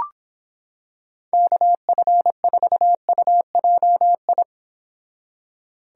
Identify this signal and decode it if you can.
Morse code, heard as a keyed tone, about 25 wpm, 700 Hz. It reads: KF4UJI